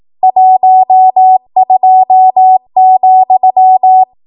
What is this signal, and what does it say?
Morse code, heard as a keyed tone, about 18 wpm, 750 Hz. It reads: 12,
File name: Morse / 18WPM